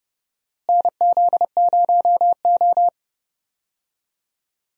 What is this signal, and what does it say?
Morse code, heard as a keyed tone, about 31 wpm, 700 Hz. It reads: NZ0O
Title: USA Call Signs - 4 Characters - Repeated 2x - 30wpm